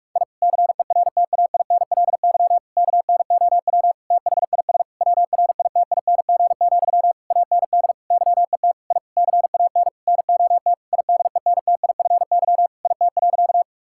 Morse code, heard as a keyed tone, about 46 wpm, 700 Hz. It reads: I CERTAINLY KNOW THIS WRITING, AND YET I CAN NOT IDENTIFY IT.
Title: Sentences from Top 1000 Words - Review Part 6 - 45wpm